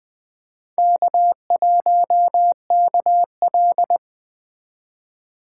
Morse code, heard as a keyed tone, about 22 wpm, 700 Hz. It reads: K1KL